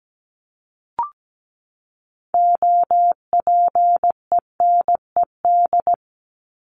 Morse code, heard as a keyed tone, about 17 wpm, 700 Hz. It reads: OPENED
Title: ED Suffix - Top 100 Words with Base Words 17wpm